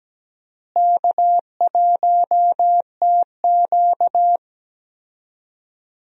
Morse code, heard as a keyed tone, about 17 wpm, 700 Hz. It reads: K1TQ
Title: USA Call Signs - 4 Characters - Repeated 2x - 17wpm